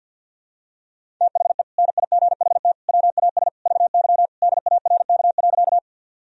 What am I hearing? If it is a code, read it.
Morse code, heard as a keyed tone, about 52 wpm, 700 Hz. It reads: THE NIGHT WAS VY DARK.
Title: Sentences from Top 500 Words and Abbreviations 50wpm